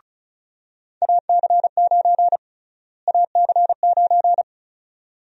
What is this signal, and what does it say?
Morse code, heard as a keyed tone, about 36 wpm, 700 Hz. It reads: AC9 AC9
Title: USA Call Sign Prefixes - Repeated 2x - 35wpm